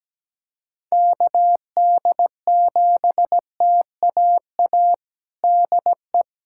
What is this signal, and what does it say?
Morse code, heard as a keyed tone, about 17 wpm, 700 Hz. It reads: KD7TAA DE
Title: USA Callsign DE Callsign 17wpm